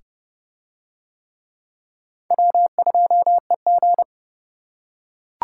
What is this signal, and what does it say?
Morse code, heard as a keyed tone, about 27 wpm, 700 Hz. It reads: W2EG